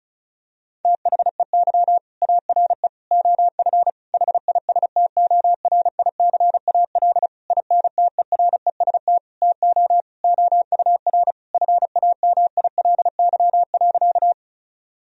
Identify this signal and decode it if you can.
Morse code, heard as a keyed tone, about 34 wpm, 700 Hz. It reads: THEY ARE OF HISTORICAL INTEREST TO OUR FAMILY.
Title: Sentences from Top 1000 Words - Review Part 4 - 35wpm